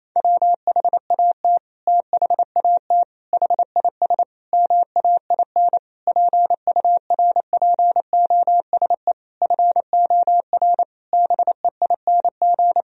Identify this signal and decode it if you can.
Morse code, heard as a keyed tone, about 28 wpm, 700 Hz. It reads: WHAT THAT HIS MAIN PURPOSE FOR BEING
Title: Sentences from Top 1000 Words - Review Part 2 - 28wpm